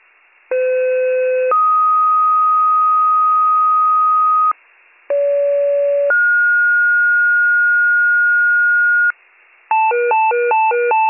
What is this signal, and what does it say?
Two-tone page, heard as a one-second tone, then a three-second tone, about 510 then 1250 Hz, about 570 then 1430 Hz, with noise. It is heard from 0.5 to 9.1 s.